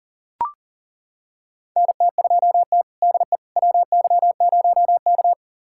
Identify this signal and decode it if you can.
Morse code, heard as a keyed tone, about 40 wpm, 700 Hz. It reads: NT2T DE WY0K
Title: USA Callsign DE Callsign - 4 Characters - 40wpm